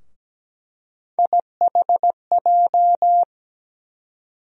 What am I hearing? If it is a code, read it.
Morse code, heard as a keyed tone, about 17 wpm, 700 Hz. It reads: IHJ
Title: USA Call Sign Postfixes - 3 Letters 17wpm